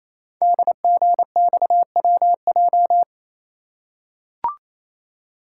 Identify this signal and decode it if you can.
Morse code, heard as a keyed tone, about 27 wpm, 700 Hz. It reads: DGXWJ
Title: Code Groups - 5 Letters - 28wpm